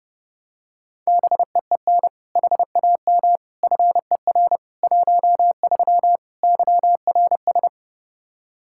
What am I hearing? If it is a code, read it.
Morse code, heard as a keyed tone, about 29 wpm, 700 Hz. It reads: BEEN HAM FER 13 YRS
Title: Practice QSO - Exchange 3 - More Info - 30wpm